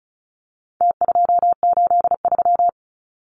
Morse code, heard as a keyed tone, about 34 wpm, 700 Hz. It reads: T283
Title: Numbers - Contest Style - 4-Digit 35wpm